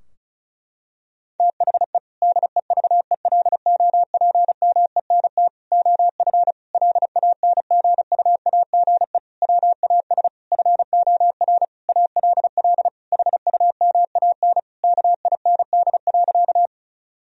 Morse code, heard as a keyed tone, about 34 wpm, 700 Hz. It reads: THE DEVELOPMENT OF LANGUAGE WAS FOR ALL HUMAN KIND.
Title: Sentences from Top 1000 Words - Review Part 4 - 35wpm